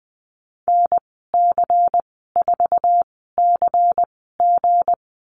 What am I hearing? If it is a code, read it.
Morse code, heard as a keyed tone, about 20 wpm, spaced as for 15 wpm, 700 Hz. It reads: NC4CG
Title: 193 Lesson 31g - Letter v - Call Signs 20wpm